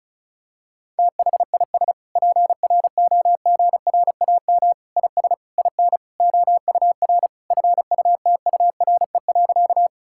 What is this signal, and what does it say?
Morse code, heard as a keyed tone, about 36 wpm, 700 Hz. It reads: THIS PROGRAM IS IN OUR FUTURE.
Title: Sentences from Top 1000 Words - Review Part 5 - 35wpm